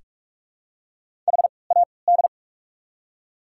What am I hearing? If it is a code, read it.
Morse code, heard as a keyed tone, about 47 wpm, spaced as for 26 wpm, 690 Hz. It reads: HAD
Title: 3x Character Spacing - Top 100 Words - 45wpm